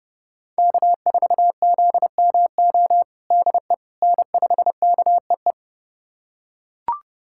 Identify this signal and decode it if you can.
Morse code, heard as a keyed tone, about 30 wpm, 700 Hz. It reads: K4ZMO DE N5KEE
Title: USA Callsign DE Callsign - 5 Characters - 30wpm